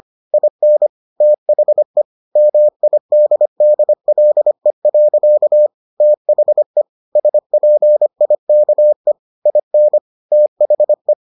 Morse code, heard as a keyed tone, about 25 wpm, 600 Hz. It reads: IN THE MIDDLE. THE SPIKE IN THE